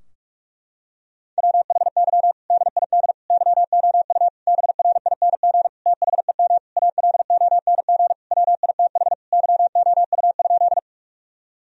Morse code, heard as a keyed tone, about 42 wpm, 700 Hz. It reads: WHY DID YOU BRING THEM ALONG WITH YOU?
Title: Sentences from Top 1000 Words - Review Part 3 - 45wpm